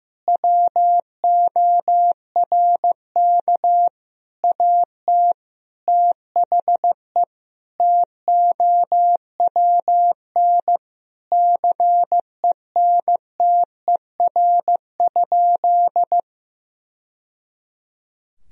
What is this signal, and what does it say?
Morse code, heard as a keyed tone, about 16 wpm, 700 Hz. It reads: WORK AT THE TOWN CENTER?
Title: Sentences from Top 1000 Words - Review Part 2 - 15wpm